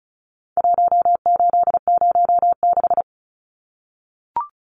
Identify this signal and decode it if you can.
Morse code, heard as a keyed tone, about 34 wpm, 700 Hz. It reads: 1806
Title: Numbers - 4-Digit 35wpm